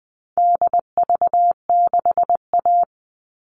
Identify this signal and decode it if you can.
Morse code, heard as a keyed tone, about 20 wpm, 700 Hz. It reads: DV6A
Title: All Cut Numbers - 4-Digit 20wpm